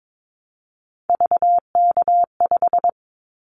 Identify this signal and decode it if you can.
Morse code, heard as a keyed tone, about 22 wpm, 700 Hz. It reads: VK5
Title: Callsign Prefixes for Top 100 DXCC - 22wpm